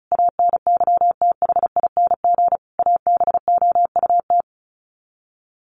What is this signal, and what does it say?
Morse code, heard as a keyed tone, about 34 wpm, 700 Hz. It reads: ANYTHING ABOUT
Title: Sets of 5 Words Encore2 35wpm